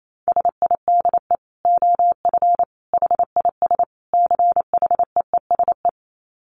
Morse code, heard as a keyed tone, about 28 wpm, 700 Hz. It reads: SIDE OF HIS CHEESE
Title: Dit Challenge 28wpm